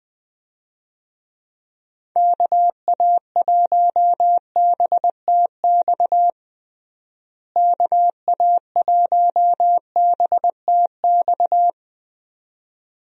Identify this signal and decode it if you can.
Morse code, heard as a keyed tone, about 20 wpm, 700 Hz. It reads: KA1BTX KA1BTX
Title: USA Call Signs - 6 Characters - Repeated 2x - 20wpm